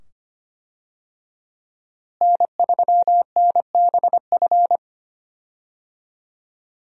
Morse code, heard as a keyed tone, about 25 wpm, 700 Hz. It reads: N3NBF
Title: USA Call Signs - 5 Characters - Repeated 2x - 25wpm